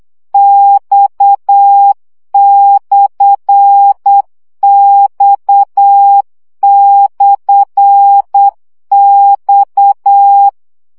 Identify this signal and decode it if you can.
Morse code, heard as a keyed tone, about 8 wpm, 800 Hz. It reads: X/X/X
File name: CW gyakorlas x per 7 baud